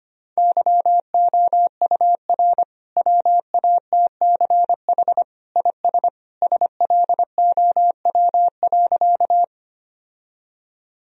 Morse code, heard as a keyed tone, about 25 wpm, 700 Hz. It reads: YOUR WATCH IS SLOW.
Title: Sentences from Top 1000 Words - Review Part 2 - 25wpm